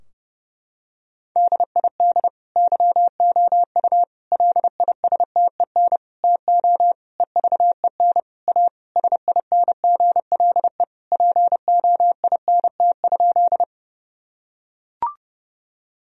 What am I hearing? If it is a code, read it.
Morse code, heard as a keyed tone, about 30 wpm, 700 Hz. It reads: DID YOU LISTEN TO EVEN A SINGLE POINT?
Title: Sentences from Top 1000 Words - Review Part 4 - 30wpm